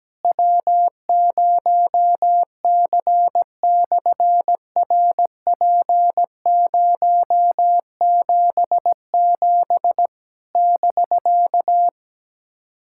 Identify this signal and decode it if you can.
Morse code, heard as a keyed tone, about 17 wpm, 700 Hz. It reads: W0C/RP077 <BK>
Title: SOTA QSO Practice 17wpm